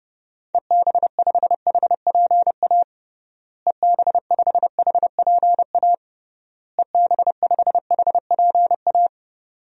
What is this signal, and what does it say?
Morse code, heard as a keyed tone, about 30 wpm, 700 Hz. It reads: EB5HPA EB5HPA EB5HPA
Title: DXCC-100 Call Signs - 6 Characters - Repeated 3x - 30wpm